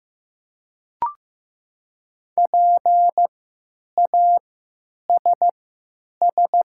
Morse code, heard as a keyed tone, about 15 wpm, spaced as for 9 wpm, 700 Hz. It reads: PASS
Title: 3x Character Spacing - Top 500 Words - 15wpm